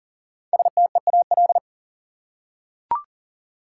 Morse code, heard as a keyed tone, about 40 wpm, 700 Hz. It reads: STEAL
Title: Five Letter Words - 40wpm